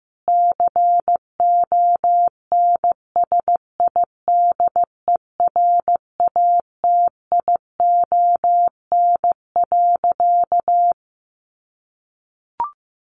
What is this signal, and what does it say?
Morse code, heard as a keyed tone, about 15 wpm, 700 Hz. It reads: CONSIDERATION.
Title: Sentences from Top 1500 Words 15wpm